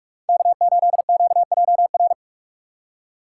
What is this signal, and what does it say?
Morse code, heard as a keyed tone, about 42 wpm, 700 Hz. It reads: K8QJR
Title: USA Call Signs - 5 Characters - 2x Character Spacing - 45wpm